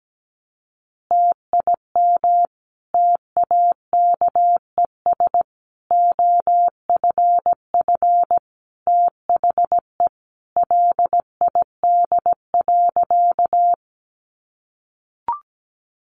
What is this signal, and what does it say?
Morse code, heard as a keyed tone, about 17 wpm, 700 Hz. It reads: TIM TAKES OFF THE LID.
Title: Easy Phrases 17wpm